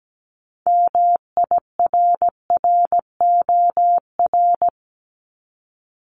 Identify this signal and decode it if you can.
Morse code, heard as a keyed tone, about 17 wpm, 700 Hz. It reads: MIRROR
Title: Multi-letter Phonemes 17wpm